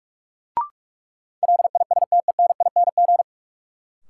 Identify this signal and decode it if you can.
Morse code, heard as a keyed tone, about 46 wpm, 700 Hz. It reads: LISTENING
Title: Sets of 5 Words - New Word Drill - 45wpm